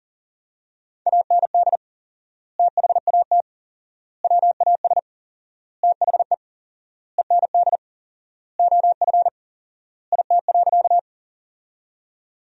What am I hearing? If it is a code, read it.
Morse code, heard as a keyed tone, about 40 wpm, 700 Hz. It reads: AND THAT WAS THE END OF IT.
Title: 4x Word-Spacing - Sentences from Top 200 Words - 40wpm